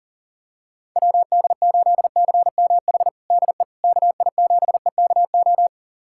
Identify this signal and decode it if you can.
Morse code, heard as a keyed tone, about 40 wpm, 700 Hz. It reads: WD8CMH DE KI7EKO
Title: USA Callsign DE Callsign 40wpm